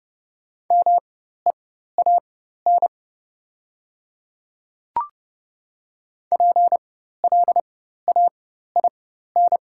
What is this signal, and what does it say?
Morse code, heard as a keyed tone, about 30 wpm, spaced as for 14 wpm, 700 Hz. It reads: MEAN PLAIN
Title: 4x Character Spacing - Top 500 Words - 30wpm